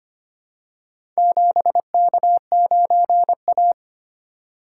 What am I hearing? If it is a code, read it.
Morse code, heard as a keyed tone, about 25 wpm, 700 Hz. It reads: 7K9A
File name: practice / 25wpm / Four Letter-Number Combinations - 25wpm